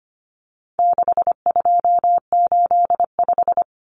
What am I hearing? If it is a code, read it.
Morse code, heard as a keyed tone, about 25 wpm, 700 Hz. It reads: 6285
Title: Common Cut Numbers - 4-Digit 25wpm